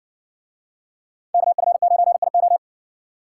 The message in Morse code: KF9IO

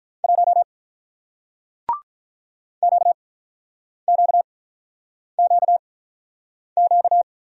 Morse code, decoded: 1 QQQQ

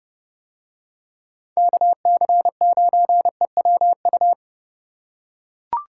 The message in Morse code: KC9EWU

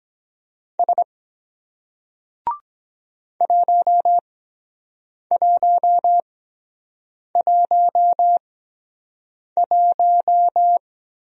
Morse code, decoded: S 1111

